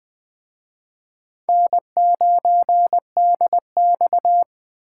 Morse code, heard as a keyed tone, about 20 wpm, 700 Hz. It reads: N9DX
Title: Speed Racing - USA 1x2 Call Signs 17wpm